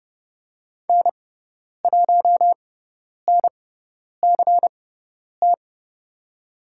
Morse code, heard as a keyed tone, about 30 wpm, spaced as for 10 wpm, 700 Hz. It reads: N1NCT
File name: Lesson 16g - Letter c - Call Signs 30wpm